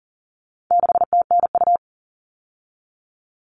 Morse code, heard as a keyed tone, about 40 wpm, 700 Hz. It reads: 6TNU
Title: All Cut Numbers - 4-Digit 40wpm